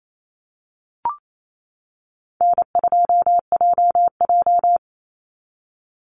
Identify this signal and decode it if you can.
Morse code, heard as a keyed tone, about 28 wpm, 700 Hz. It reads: N2JJ
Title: Call Signs - 28wpm